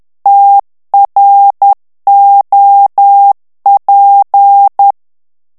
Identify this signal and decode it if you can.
Morse code, heard as a keyed tone, about 10 wpm, 800 Hz. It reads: TROP
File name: morse mp3 / FRANCAIS TEXTE CLAIR 2 VITESSE 10 MOTS MINUTES